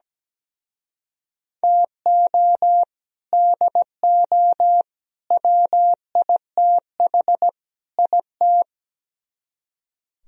Morse code, heard as a keyed tone, about 19 wpm, 700 Hz. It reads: TO DO WITH IT
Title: Speed-Racing - Sets of 4 Words 17wpm